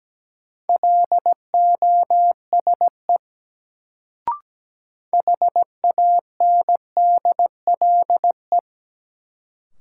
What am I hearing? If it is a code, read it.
Morse code, heard as a keyed tone, about 17 wpm, 700 Hz. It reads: LOSE HANDLE
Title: Sentences from Top 1100 Words - New Word Drill Alt - 17wpm